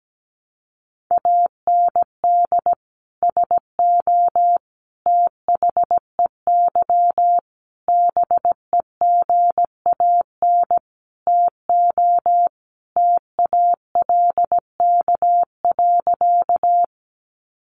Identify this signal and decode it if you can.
Morse code, heard as a keyed tone, about 17 wpm, 700 Hz. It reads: AND SO THEY BEGAN TO TALK.
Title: Sentences from Top 400 Words 17wpm